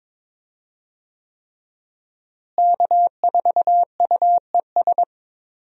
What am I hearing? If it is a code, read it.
Morse code, heard as a keyed tone, about 22 wpm, 700 Hz. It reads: K4UES
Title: USA Call Signs - 5 Characters - Repeated 3x - 22wpm